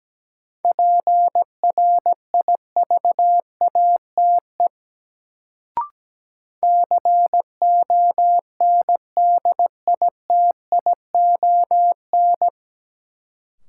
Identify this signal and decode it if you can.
Morse code, heard as a keyed tone, about 17 wpm, 700 Hz. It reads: PRIVATE CONDITION
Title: Sentences from Top 1000 Words - New Word Drill - 17wpm